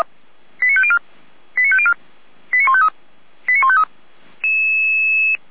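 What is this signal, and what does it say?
Five-tone selective call (ZVEI), heard as quick runs of five tones, tones of about 70 ms. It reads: 89504 89504 89153 89153
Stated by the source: ZVEI five-tone selective call received by SDR.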